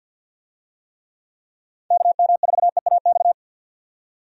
Morse code, heard as a keyed tone, about 46 wpm, 700 Hz. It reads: KM4EAX
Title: USA Call Signs - 6 Characters - 50wpm